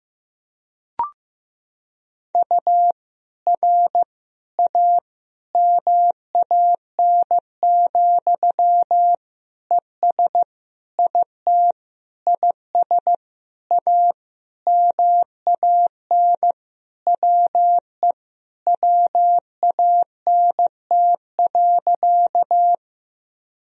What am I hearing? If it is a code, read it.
Morse code, heard as a keyed tone, about 15 wpm, 700 Hz. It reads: U R A MAN, ES IT IS A MAN WE WANT.
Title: Sentences from Top 200 Words and Abbreviations 15wpm